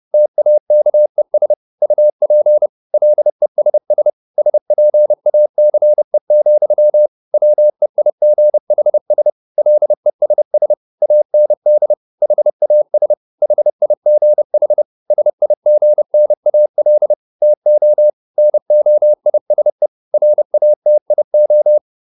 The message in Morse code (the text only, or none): TAKES UP LESS SPACE, WEIGHS LESS AND HAS HIGH SIGNAL TO NOISE RATIO